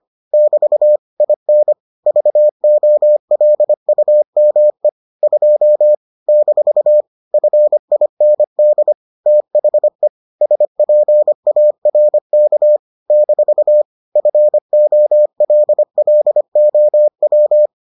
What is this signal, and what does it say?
Morse code, heard as a keyed tone, about 25 wpm, 600 Hz. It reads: <BT> IN VOLUME 2 - FIND THE SPARK - FOLLOW